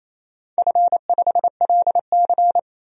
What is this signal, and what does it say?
Morse code, heard as a keyed tone, about 28 wpm, 700 Hz. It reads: F5LC
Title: DXCC-100 Call Signs - 4 Characters - 28wpm